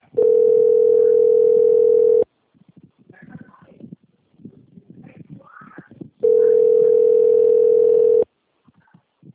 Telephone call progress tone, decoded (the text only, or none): ringback tone